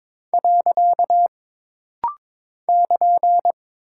Morse code, heard as a keyed tone, about 22 wpm, 700 Hz. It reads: . <KN>
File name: practice / 22wpm / Single Punctuation-Prosign - Rapid-Fire - 22wpm